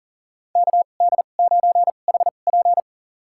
Morse code, heard as a keyed tone, about 40 wpm, spaced as for 29 wpm, 700 Hz. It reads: KD9HP
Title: USA Call Signs - 5 Characters - 2x Character Spacing - 40wpm